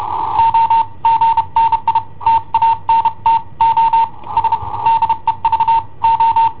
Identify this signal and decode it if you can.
Morse code, heard as a keyed tone, about 31 wpm, spaced as for 26 wpm, 910 Hz. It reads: OGNI TANTO S DEVO
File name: i7ohp / cassetta i7ohp B